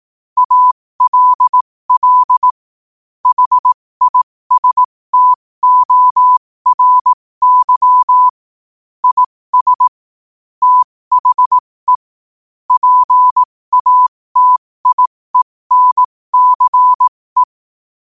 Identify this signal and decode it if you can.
Morse code, heard as a keyed tone, about 18 wpm, spaced as for 15 wpm, 1000 Hz. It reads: ALL HISTORY IS THE PATIENCE